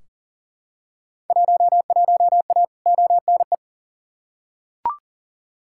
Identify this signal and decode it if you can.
Morse code, heard as a keyed tone, about 40 wpm, 700 Hz. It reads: 11A ONE
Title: Field Day Exchanges Encore - 40wpm